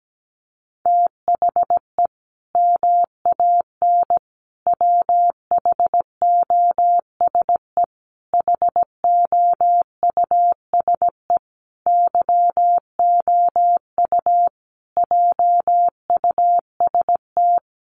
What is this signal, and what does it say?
Morse code, heard as a keyed tone, about 17 wpm, 700 Hz. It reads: THE MAN WHOSE HOUSE YOU JUST
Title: Sentences from Top 1200 Words 17wpm